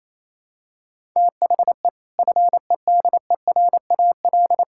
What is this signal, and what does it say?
Morse code, heard as a keyed tone, about 28 wpm, 700 Hz. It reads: THE FEDERAL
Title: Sentences from Top 1000 Words - Review Part 5 - 28wpm